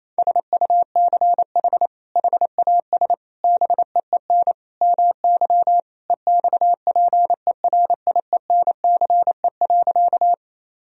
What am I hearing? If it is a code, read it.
Morse code, heard as a keyed tone, about 28 wpm, 700 Hz. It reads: SUCH HAS BEEN MY EXPERIENCE.
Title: Sentences from Top 1000 Words - Review Part 3 - 28wpm